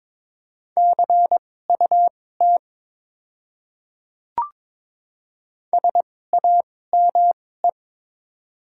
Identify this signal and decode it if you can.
Morse code, heard as a keyed tone, about 22 wpm, spaced as for 16 wpm, 700 Hz. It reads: CUT SAME
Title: 2x Character Spacing - Top 300 Words - 22wpm